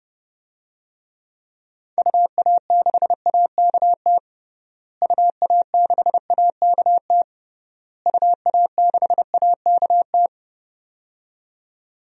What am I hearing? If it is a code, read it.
Morse code, heard as a keyed tone, about 30 wpm, 700 Hz. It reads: UA6AKT UA6AKT UA6AKT
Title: DXCC-100 Call Signs - 6 Characters - Repeated 3x - 30wpm